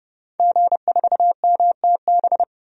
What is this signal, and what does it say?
Morse code, heard as a keyed tone, about 30 wpm, 700 Hz. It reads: G4MTB
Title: DXCC-100 Call Signs - 5 Characters - 30wpm